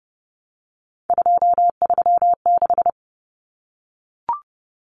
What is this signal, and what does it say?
Morse code, heard as a keyed tone, about 30 wpm, 700 Hz. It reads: 236
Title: Numbers - 3-Digit 30wpm